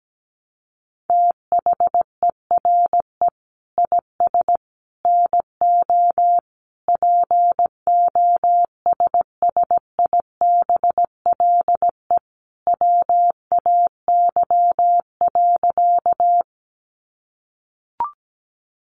Morse code, heard as a keyed tone, about 17 wpm, 700 Hz. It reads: THERE IS NO POSSIBLE WAY.
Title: Sentences from Top 500 Words 17wpm